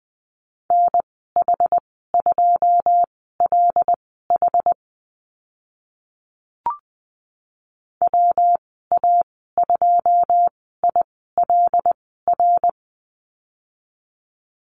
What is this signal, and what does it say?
Morse code, heard as a keyed tone, about 20 wpm, spaced as for 15 wpm, 700 Hz. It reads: NH2LH WA2ILR
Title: 122 Lesson 20g - Number 3 - Call Signs 20wpm